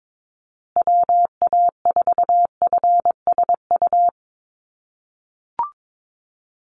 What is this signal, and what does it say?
Morse code, heard as a keyed tone, about 22 wpm, 700 Hz. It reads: WA4FSU
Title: USA Call Signs Repeated 2x 22wpm